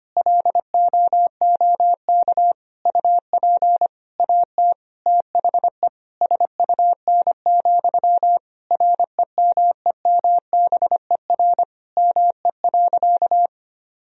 Morse code, heard as a keyed tone, about 25 wpm, 700 Hz. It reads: LOOK UP AT THE SUN, REMEMBER ME.
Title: Sentences from Top 1000 Words - Review Part 1 - 25wpm